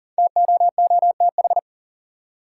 Morse code, heard as a keyed tone, about 40 wpm, 700 Hz. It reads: TOOTH
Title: Five Letter Words - 40wpm